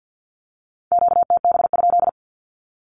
Morse code, heard as a keyed tone, about 48 wpm, 700 Hz. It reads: QTB?
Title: Q-Codes Repeated 3x 50wpm